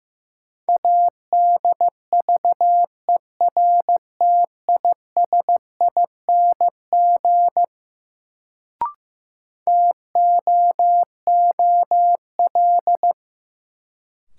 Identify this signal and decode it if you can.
Morse code, heard as a keyed tone, about 15 wpm, 700 Hz. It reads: ADVERTISING TOOL
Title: Sentences from Top 1000 Words - Review Part 6 - New Word Drill - 15wpm